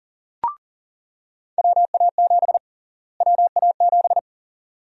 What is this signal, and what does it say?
Morse code, heard as a keyed tone, about 40 wpm, 700 Hz. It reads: WA7 WA7
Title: USA Call Sign Prefixes - Repeated 2x - 40wpm